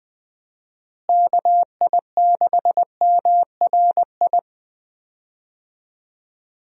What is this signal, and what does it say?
Morse code, heard as a keyed tone, about 21 wpm, 700 Hz. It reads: KI6MRI
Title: USA Call Signs - 6 Characters - Repeated 3x - 20wpm